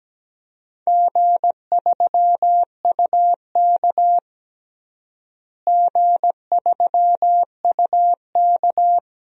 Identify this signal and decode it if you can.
Morse code, heard as a keyed tone, about 17 wpm, 700 Hz. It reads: G3UK G3UK
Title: DXCC-100 Call Signs - 4 Characters - Repeated 3x - 17wpm